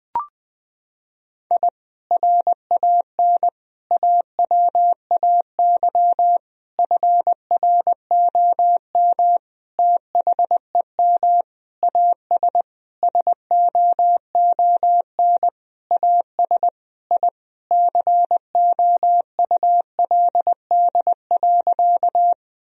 Morse code, heard as a keyed tone, about 20 wpm, 700 Hz. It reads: I RAN AWAY FROM THEM AS SOON AS I COULD.
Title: Sentences from Top 1000 Words - Review Part 3 - 20wpm